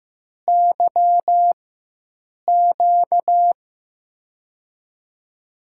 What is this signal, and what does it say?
Morse code, heard as a keyed tone, about 15 wpm, spaced as for 7 wpm, 700 Hz. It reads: YQ
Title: USA Call Signs - 4x Character Spacing - 15wpm